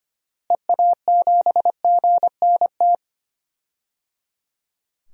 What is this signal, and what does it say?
Morse code, heard as a keyed tone, about 25 wpm, 700 Hz. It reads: EA7GNT
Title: DXCC-100 Call Signs - 6 Characters - Repeated 3x - 25wpm